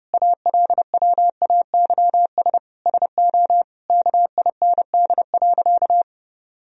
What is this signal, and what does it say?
Morse code, heard as a keyed tone, about 30 wpm, 700 Hz. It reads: ALWAYS SO KIND.
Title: Sentences from Top 1000 Words - Review Part 3 - 30wpm